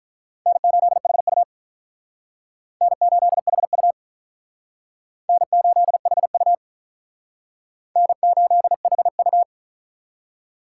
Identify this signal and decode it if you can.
Morse code, heard as a keyed tone, about 45 wpm, 700 Hz. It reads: N8HU N8HU N8HU N8HU